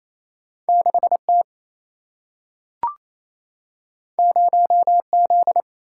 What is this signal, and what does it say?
Morse code, heard as a keyed tone, about 28 wpm, 700 Hz. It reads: 6T 0Z
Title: Single-Letter Single-Number 28wpm